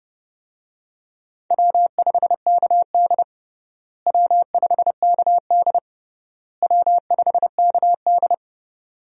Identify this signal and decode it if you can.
Morse code, heard as a keyed tone, about 30 wpm, 700 Hz. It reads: W5KD W5KD W5KD
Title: USA Call Signs - 4 Characters - Repeated 3x - 30wpm